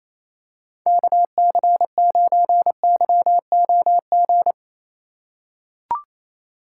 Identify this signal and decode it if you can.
Morse code, heard as a keyed tone, about 28 wpm, 700 Hz. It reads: KC9YOG
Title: USA Call Signs - 6 Characters - 4x Character Spacing - 28wpm